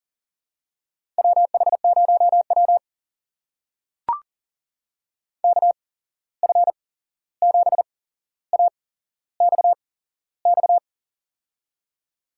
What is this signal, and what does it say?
Morse code, heard as a keyed tone, about 38 wpm, spaced as for 15 wpm, 700 Hz. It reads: WH0W KF7AXX
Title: USA Call Signs - 8x Character Spacing - 40wpm